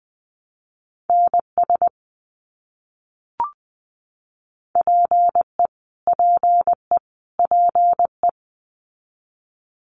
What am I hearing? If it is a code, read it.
Morse code, heard as a keyed tone, about 20 wpm, 700 Hz. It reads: NS PE PE PE
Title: Canadian Provinces and Territories - Repeated 3x - 20wpm